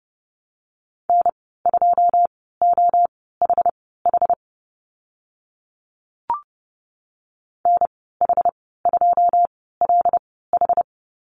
Morse code, heard as a keyed tone, about 30 wpm, spaced as for 17 wpm, 700 Hz. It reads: N2OHH NH2LH
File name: Lesson 15g - Number 5 - Call Signs 30wpm